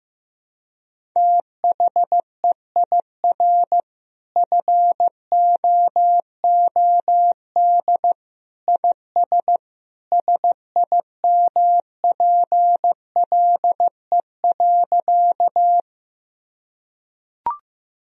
Morse code, heard as a keyed tone, about 15 wpm, 700 Hz. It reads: THEIR FOOD IS SIMPLE.
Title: Sentences from Top 1000 Words - Review Part 1 - 15wpm